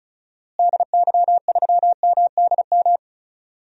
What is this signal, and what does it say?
Morse code, heard as a keyed tone, about 36 wpm, 700 Hz. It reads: DY3MDM
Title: DXCC-100 Call Signs - 6 Characters - Repeated 2x - 35wpm